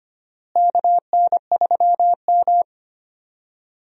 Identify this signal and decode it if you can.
Morse code, heard as a keyed tone, about 25 wpm, 700 Hz. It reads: KN3M
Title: USA Call Signs - 4 Characters - Repeated 2x - 25wpm